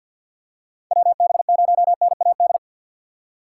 Morse code, heard as a keyed tone, about 49 wpm, 700 Hz. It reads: WB0NAD
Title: USA Call Signs - 6 Characters - 6x Character Spacing - 50wpm